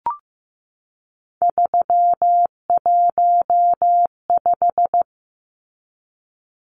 Morse code, heard as a keyed tone, about 14 wpm, 700 Hz. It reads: 315
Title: Numbers - 3-Digit 15wpm